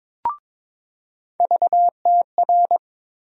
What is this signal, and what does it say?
Morse code, heard as a keyed tone, about 22 wpm, 700 Hz. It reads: VTR
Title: USA Call Sign Postfixes - 3 Letters 22wpm